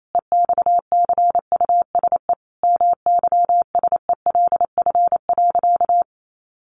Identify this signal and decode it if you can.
Morse code, heard as a keyed tone, about 28 wpm, 700 Hz. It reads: EXCUSE MYSELF.
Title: Sentences from Top 1300 Words 28wpm